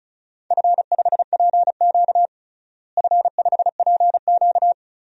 Morse code, heard as a keyed tone, about 34 wpm, 700 Hz. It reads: F5PQ F5PQ
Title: DXCC-100 Call Signs - 4 Characters - Repeated 3x - 35wpm